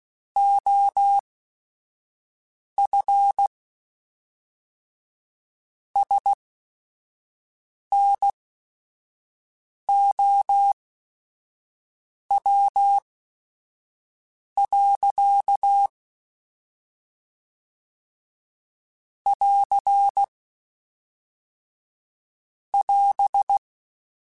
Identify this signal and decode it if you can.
Morse code, heard as a keyed tone, about 16 wpm, spaced as for 5 wpm, 800 Hz. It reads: OF SNOW. <AR> <AS>
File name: Weather Sets / Weather Sets-mp3s / Weather 3 17 @5